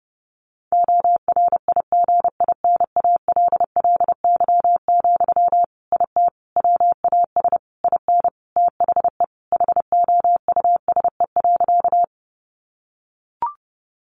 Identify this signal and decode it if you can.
Morse code, heard as a keyed tone, about 30 wpm, 700 Hz. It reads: ORIGINALLY, IT WAS IN THE HOUSE.
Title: Sentences from Top 1500 Words 30wpm